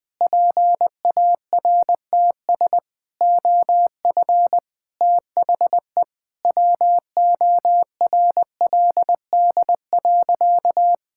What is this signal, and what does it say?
Morse code, heard as a keyed tone, about 20 wpm, 700 Hz. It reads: PARTS OF THE WORLD.
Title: Sentences with 3-Word Phrases - 20wpm